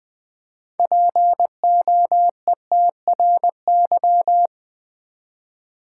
Morse code, heard as a keyed tone, about 20 wpm, 700 Hz. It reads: POETRY
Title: Six Letter Words - 20wpm